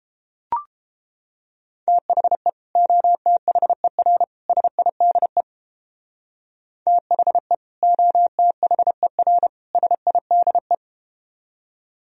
Morse code, heard as a keyed tone, about 31 wpm, 700 Hz. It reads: THE OTHER SIDE THE OTHER SIDE